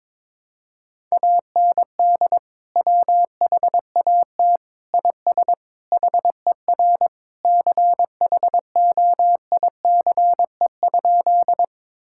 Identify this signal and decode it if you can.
Morse code, heard as a keyed tone, about 22 wpm, 700 Hz. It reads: AND WHAT IS HER CHOICE?
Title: Sentences from Top 1000 Words - Review Part 6 - 22wpm